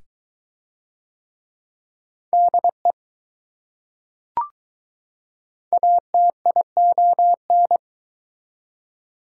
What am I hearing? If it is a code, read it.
Morse code, heard as a keyed tone, about 20 wpm, 700 Hz. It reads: DE ATION